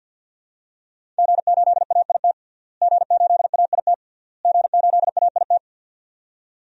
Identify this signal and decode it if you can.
Morse code, heard as a keyed tone, about 52 wpm, 700 Hz. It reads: G8AIT G8AIT G8AIT
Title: DXCC-100 Call Signs - 5 Characters - Repeated 3x - 50wpm